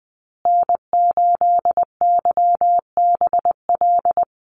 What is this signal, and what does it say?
Morse code, heard as a keyed tone, about 20 wpm, 700 Hz. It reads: N8YBL
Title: USA Call Signs Repeated 3x 20wpm